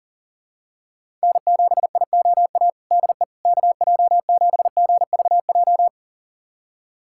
Morse code, heard as a keyed tone, about 37 wpm, 700 Hz. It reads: N7IOA DE KJ7GVJ
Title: USA Callsign DE Callsign 40wpm